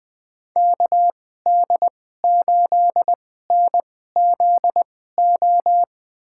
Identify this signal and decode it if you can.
Morse code, heard as a keyed tone, about 21 wpm, spaced as for 15 wpm, 700 Hz. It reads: KD8NZO
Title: USA Call Signs - 6 Characters - 2x Character Spacing - 20wpm